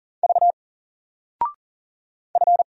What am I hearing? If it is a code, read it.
Morse code, heard as a keyed tone, about 40 wpm, 700 Hz. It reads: V F
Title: Single Character (Letter:Number:Punctuation:Prosign) 40wpm